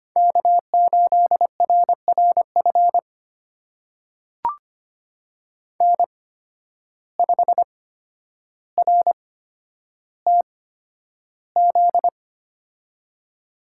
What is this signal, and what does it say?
Morse code, heard as a keyed tone, about 25 wpm, spaced as for 11 wpm, 700 Hz. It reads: K8RRF N5RTZ
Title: USA Call Signs - 5 Characters - 8x Character Spacing - 25wpm